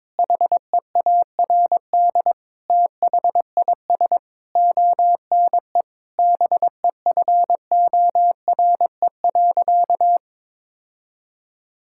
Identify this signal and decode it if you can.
Morse code, heard as a keyed tone, about 21 wpm, 700 Hz. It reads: HEARD THIS ONE BEFORE.
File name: practice / 22wpm / Sentences with 8-Word Phrases - 22wpm